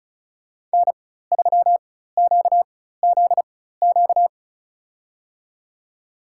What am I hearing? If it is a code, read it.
Morse code, heard as a keyed tone, about 36 wpm, spaced as for 16 wpm, 700 Hz. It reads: N3QZQ
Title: USA Call Signs - 5 Characters - 4x Character Spacing - 35wpm